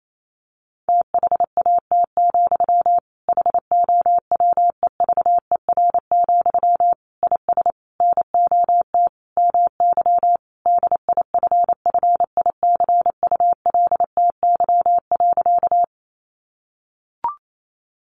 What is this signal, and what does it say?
Morse code, heard as a keyed tone, about 28 wpm, 700 Hz. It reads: THAT, HOWEVER, IS NOT MY DIFFICULTY.
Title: Sentences from Top 1500 Words 28wpm